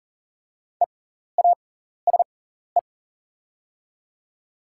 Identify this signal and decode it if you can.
Morse code, heard as a keyed tone, about 40 wpm, spaced as for 14 wpm, 690 Hz. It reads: EASE
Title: 6x Character Spacing - Top 300 Words - 40wpm